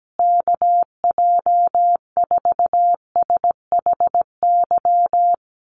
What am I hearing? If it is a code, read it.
Morse code, heard as a keyed tone, about 17 wpm, 700 Hz. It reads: KJ4SHY
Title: USA Call Signs Repeated 3x 17wpm